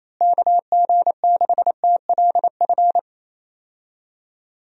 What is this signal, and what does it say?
Morse code, heard as a keyed tone, about 28 wpm, 700 Hz. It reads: KG6TLF
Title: USA Call Signs - 6 Characters - Repeated 3x - 28wpm